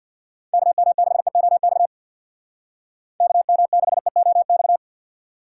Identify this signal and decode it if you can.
Morse code, heard as a keyed tone, about 62 wpm, 700 Hz. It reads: KM6EOX KM6EOX
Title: Speed Racing - USA 2x3 Call Signs 50wpm